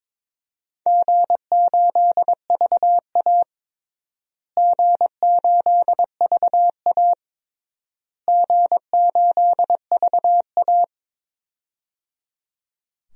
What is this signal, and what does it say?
Morse code, heard as a keyed tone, about 22 wpm, 700 Hz. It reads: G8VA G8VA G8VA
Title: DXCC-100 Call Signs - 4 Characters - Repeated 3x - 22wpm